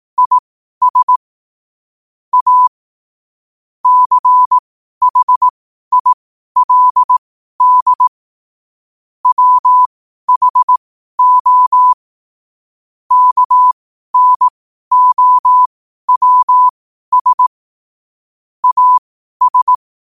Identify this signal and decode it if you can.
Morse code, heard as a keyed tone, about 18 wpm, spaced as for 12 wpm, 1000 Hz. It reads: IS A CHILD WHO KNOWS AS
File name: qotdfiles / 12wpm